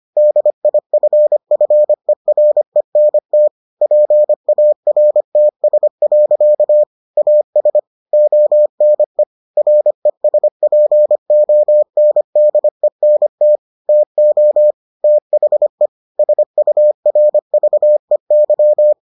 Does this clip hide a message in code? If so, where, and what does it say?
Yes: Morse code from 0.2 to 18.9 s, heard as a keyed tone, about 25 wpm, 600 Hz. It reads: DIFFERENT PARTS. AS ONE RESPONDENT TO THE SURVEY